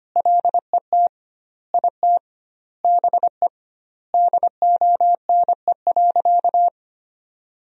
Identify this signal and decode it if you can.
Morse code, heard as a keyed tone, about 25 wpm, 700 Hz. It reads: LET IT BE DONE.
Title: 2x Word-Spacing - Sentences from Top 500 Words - 25wpm